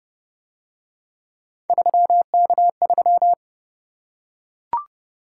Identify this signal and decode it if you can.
Morse code, heard as a keyed tone, about 28 wpm, 700 Hz. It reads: 3K3